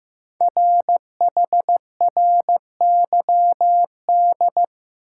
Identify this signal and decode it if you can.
Morse code, heard as a keyed tone, about 15 wpm, 700 Hz. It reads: RHRYD
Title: Code Groups - 5 Letters - 15wpm